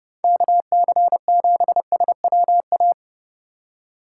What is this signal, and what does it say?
Morse code, heard as a keyed tone, about 29 wpm, 700 Hz. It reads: KC7SWA
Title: USA Call Signs - 6 Characters - 4x Character Spacing - 30wpm